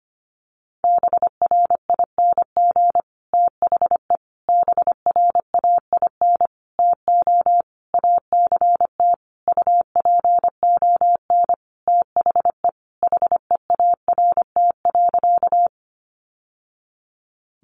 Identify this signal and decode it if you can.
Morse code, heard as a keyed tone, about 25 wpm, 700 Hz. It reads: BRING THE BRAIN TO ACT UPON THE HEART.
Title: Sentences from Top 1200 Words 25wpm